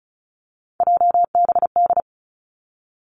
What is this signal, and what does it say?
Morse code, heard as a keyed tone, about 36 wpm, 700 Hz. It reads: JBD
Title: Three Challenging Letter-Number Combinations 35wpm